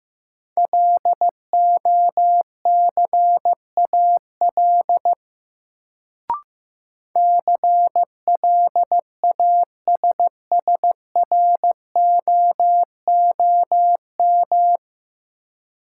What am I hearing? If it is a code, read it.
Morse code, heard as a keyed tone, about 15 wpm, 700 Hz. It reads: LOCAL CLASSROOM
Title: Sentences from Top 1500 Words - New Word Drill Alt - 15wpm